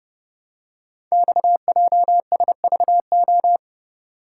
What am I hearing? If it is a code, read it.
Morse code, heard as a keyed tone, about 29 wpm, 700 Hz. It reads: XJSVO
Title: Code Groups - 5 Letters - 30wpm